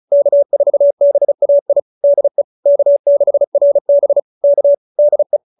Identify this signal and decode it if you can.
Morse code, heard as a keyed tone, about 34 wpm, 570 Hz. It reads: K4BAI DE K6RB K DE